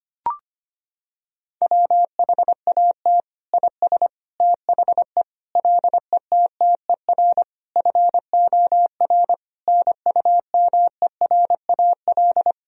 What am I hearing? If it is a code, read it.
Morse code, heard as a keyed tone, about 25 wpm, 700 Hz. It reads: WHAT IS THE LETTER FOR NUMERAL
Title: Sentences from Top 1000 Words - Review Part 2 - 25wpm